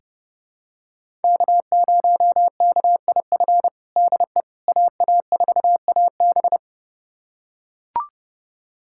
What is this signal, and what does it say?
Morse code, heard as a keyed tone, about 30 wpm, 700 Hz. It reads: K0KIF DE AA4AB
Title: USA Callsign DE Callsign - 5 Characters - 30wpm